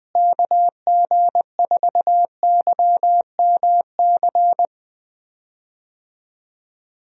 Morse code, heard as a keyed tone, about 21 wpm, 700 Hz. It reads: KG4YMC
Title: USA Call Signs - 6 Characters - Repeated 2x - 20wpm